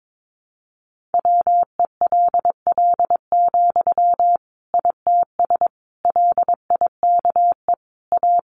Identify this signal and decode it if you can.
Morse code, heard as a keyed tone, about 22 wpm, 700 Hz. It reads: WELL, ITS LIKE A
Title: Sentences from Top 600 Words 22wpm